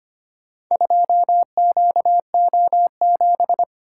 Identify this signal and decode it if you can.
Morse code, heard as a keyed tone, about 25 wpm, 700 Hz. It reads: 2QO7
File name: practice / 25wpm / Four Letter-Number Combinations - 25wpm